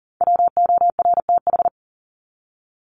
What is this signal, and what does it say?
Morse code, heard as a keyed tone, about 40 wpm, 700 Hz. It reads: WORTH
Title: Top 501-750 Words 40wpm